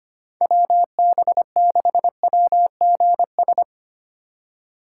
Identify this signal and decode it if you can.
Morse code, heard as a keyed tone, about 27 wpm, 700 Hz. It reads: WB6WGS
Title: Speed Racing - USA 2x3 Call Signs 25wpm